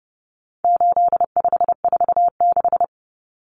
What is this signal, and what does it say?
Morse code, heard as a keyed tone, about 29 wpm, 700 Hz. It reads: 8546
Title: Common Cut Numbers - 4-Digit 30wpm